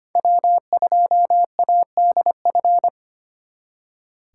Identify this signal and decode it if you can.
Morse code, heard as a keyed tone, about 32 wpm, 700 Hz. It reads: W2ADF